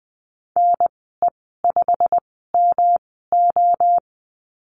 Morse code, heard as a keyed tone, about 20 wpm, spaced as for 15 wpm, 700 Hz. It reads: NE5MO
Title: 140 Lesson 23g - Letter f - Call Signs 20wpm